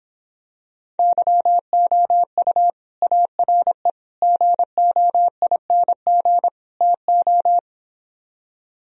Morse code, heard as a keyed tone, about 26 wpm, 700 Hz. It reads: YOU ARE GOING TO